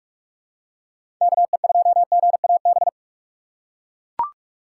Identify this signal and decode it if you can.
Morse code, heard as a keyed tone, about 42 wpm, 700 Hz. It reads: KE2GAB
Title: USA Call Signs - 6 Characters - 3x Character Spacing - 45wpm